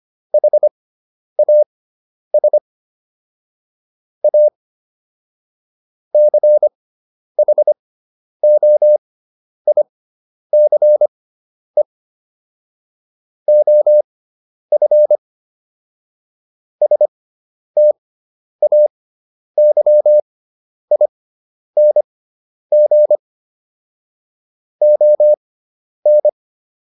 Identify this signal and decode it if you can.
Morse code, heard as a keyed tone, about 25 wpm, spaced as for 10 wpm, 600 Hz. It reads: HAS A CHOICE OF STAYING ON